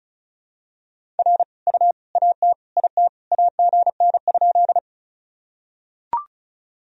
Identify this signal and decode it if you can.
Morse code, heard as a keyed tone, about 36 wpm, 700 Hz. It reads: R U AT IT AGN?
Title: Sentences from Top 200 Words and Abbreviations 35wpm